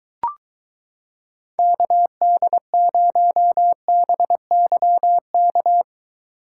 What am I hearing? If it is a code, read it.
Morse code, heard as a keyed tone, about 23 wpm, 700 Hz. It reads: KD0BYK